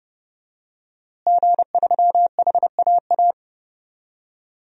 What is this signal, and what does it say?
Morse code, heard as a keyed tone, about 32 wpm, 700 Hz. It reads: G3HAA
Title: DXCC-100 Call Signs - 5 Characters - Repeated 3x - 30wpm